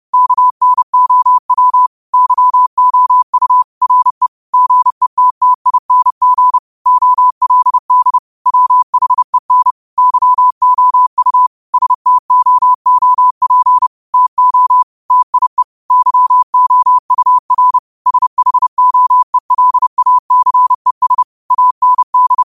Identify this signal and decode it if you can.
Morse code, heard as a keyed tone, about 30 wpm, 1000 Hz. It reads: KNOW YOU RE GETTING OLD WHEN YOU STOOP TO TIE YOUR SHOELACES AND